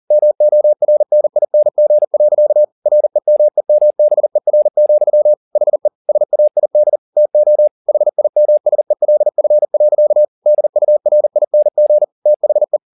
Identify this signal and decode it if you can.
Morse code, heard as a keyed tone, about 40 wpm, 600 Hz. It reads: MORNING. REMEMBER, HE SAID TO HIMSELF. DURING THE